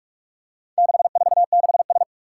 Morse code, heard as a keyed tone, about 44 wpm, 700 Hz. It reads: 646S
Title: Four Challenging Letter-Number Combinations 45wpm